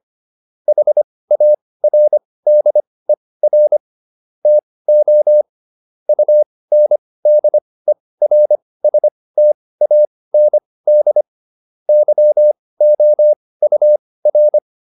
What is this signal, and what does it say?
Morse code, heard as a keyed tone, about 25 wpm, spaced as for 18 wpm, 600 Hz. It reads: HARDER TO UNDERSTAND YOUR